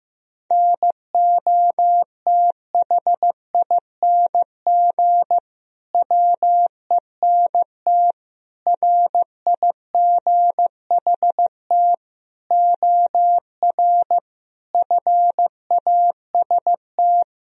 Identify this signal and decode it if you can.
Morse code, heard as a keyed tone, about 15 wpm, 700 Hz. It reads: NOTHING WENT RIGHT OR FAST
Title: Sentences from Top 1000 Words - Review Part 2 - 15wpm